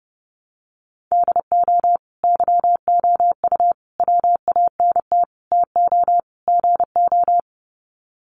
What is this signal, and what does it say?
Morse code, heard as a keyed tone, about 30 wpm, 700 Hz. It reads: DO YOU WANT TO GO
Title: Sets of 5 Words Encore2 30wpm